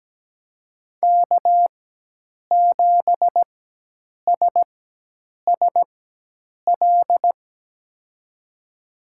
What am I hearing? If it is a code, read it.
Morse code, heard as a keyed tone, about 17 wpm, spaced as for 8 wpm, 700 Hz. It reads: K7SSL
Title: USA Call Signs - 4x Character Spacing - 17wpm